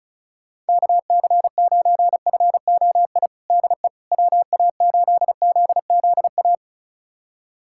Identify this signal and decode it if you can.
Morse code, heard as a keyed tone, about 36 wpm, 700 Hz. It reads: KC9FOI DE WA8ZZA
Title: USA Callsign DE Callsign - 6 Characters - 35wpm